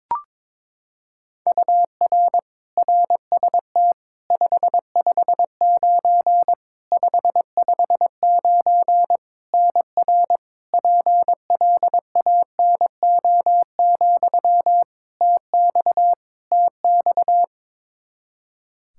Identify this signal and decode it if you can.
Morse code, heard as a keyed tone, about 22 wpm, 700 Hz. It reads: UR RST 559 559 NR PLANO, TX TX
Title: Practice QSO - Exchange 1 - Beginning - 22wpm